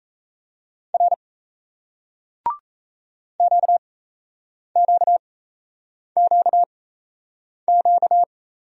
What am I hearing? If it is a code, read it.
Morse code, heard as a keyed tone, about 37 wpm, spaced as for 8 wpm, 700 Hz. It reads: R QQQQ